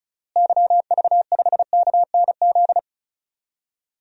Morse code, heard as a keyed tone, about 34 wpm, 700 Hz. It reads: YV5KNZ
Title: DXCC-100 Call Signs - 6 Characters - Repeated 3x - 35wpm